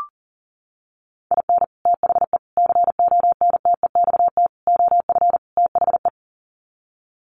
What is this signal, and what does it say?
Morse code, heard as a keyed tone, about 40 wpm, 700 Hz. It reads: IN THE CONTEXT OF THE
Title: Sets of 5 Words 40wpm